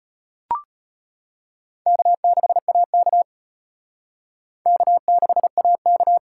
Morse code, heard as a keyed tone, about 37 wpm, 700 Hz. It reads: K6AK K6AK